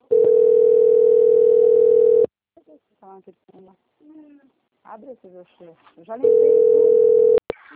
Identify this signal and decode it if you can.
Ringback tone (North America), 440 + 480 Hz, with speech.
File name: out- - - - - .40533